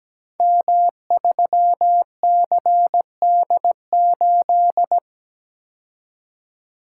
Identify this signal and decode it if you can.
Morse code, heard as a keyed tone, about 17 wpm, 700 Hz. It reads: M3CD8
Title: Code Groups - 5 Letter:Numbers - 17wpm